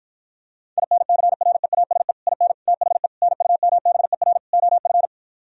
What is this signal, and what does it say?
Morse code, heard as a keyed tone, about 43 wpm, 700 Hz. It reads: INCREASE IN THE NUMBER OF
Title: Speed-Racing - Sets of 5 Words 35wpm